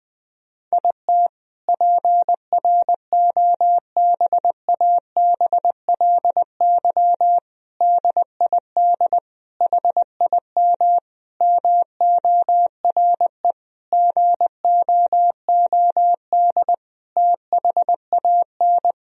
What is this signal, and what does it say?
Morse code, heard as a keyed tone, about 21 wpm, 700 Hz. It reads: IT PROBABLY DID HIM MORE GOOD THAN
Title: Sentences from Top 1000 Words - Review Part 4 - 20wpm